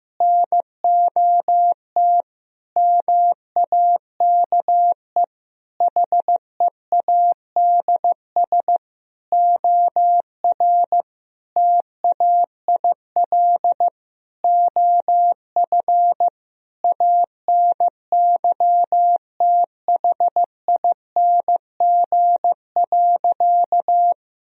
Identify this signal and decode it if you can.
Morse code, heard as a keyed tone, about 15 wpm, 700 Hz. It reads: NOT MAKE HEADS OR TAIL OF ANYTHING.
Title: Sentences from Top 1000 Words - Review Part 3 - 15wpm